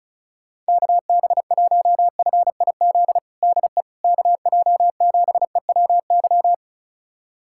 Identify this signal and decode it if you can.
Morse code, heard as a keyed tone, about 36 wpm, 700 Hz. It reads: KB1FIZ DE KJ7EWY
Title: USA Callsign DE Callsign 35wpm